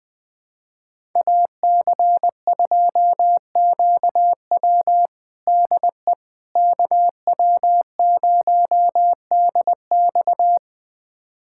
Morse code, heard as a keyed tone, about 20 wpm, 700 Hz. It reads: AC2QW DE KW0DX
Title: USA Callsign DE Callsign - 5 Characters - 20wpm